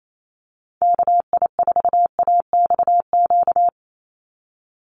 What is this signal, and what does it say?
Morse code, heard as a keyed tone, about 28 wpm, 700 Hz. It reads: KI4AXQ
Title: USA Call Signs 28wpm